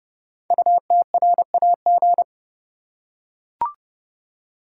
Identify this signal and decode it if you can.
Morse code, heard as a keyed tone, about 28 wpm, 700 Hz. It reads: UTRAG
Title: Code Groups - 5 Letter:Numbers - 30wpm